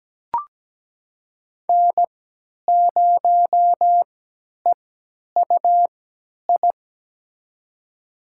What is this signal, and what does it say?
Morse code, heard as a keyed tone, about 17 wpm, spaced as for 10 wpm, 700 Hz. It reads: N0EUI
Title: USA Call Signs - 5 Characters - 3x Character Spacing - 17wpm